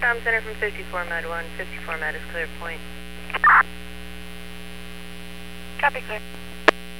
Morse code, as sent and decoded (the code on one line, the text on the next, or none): none